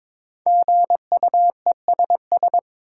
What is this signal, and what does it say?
Morse code, heard as a keyed tone, about 22 wpm, 700 Hz. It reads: GUESS
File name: 5 Letter Words - First 3 Letters Given - 22wpm